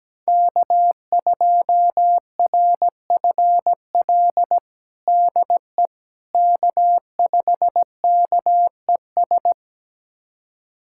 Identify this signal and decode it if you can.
Morse code, heard as a keyed tone, about 17 wpm, 700 Hz. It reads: K2RFL DE K5KES
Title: USA Callsign DE Callsign - 5 Characters - 17wpm